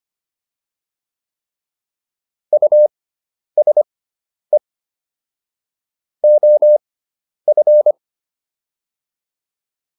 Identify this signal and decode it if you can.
Morse code, heard as a keyed tone, about 25 wpm, spaced as for 10 wpm, 600 Hz. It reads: USE OF